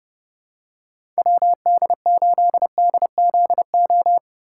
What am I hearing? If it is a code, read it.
Morse code, heard as a keyed tone, about 30 wpm, 700 Hz. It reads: WD8DZO
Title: USA Call Signs - 6 Characters - 2x Character Spacing - 30wpm